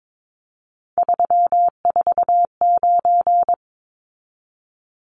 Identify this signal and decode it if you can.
Morse code, heard as a keyed tone, about 22 wpm, 700 Hz. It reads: 349
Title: Numbers - 3-Digit 22wpm